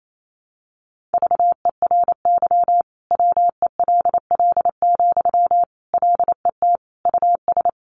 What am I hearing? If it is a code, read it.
Morse code, heard as a keyed tone, about 28 wpm, 700 Hz. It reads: VERY WELL, LET US
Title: Sentences from Top 1200 Words 28wpm